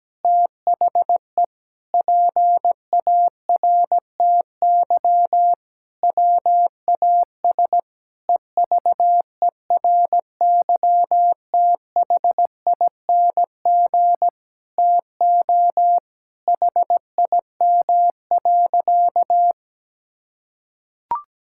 Morse code, heard as a keyed tone, about 17 wpm, 700 Hz. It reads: THE PARTY WAS EVERYTHING TO HIM.
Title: Sentences from Top 1600 Words 17wpm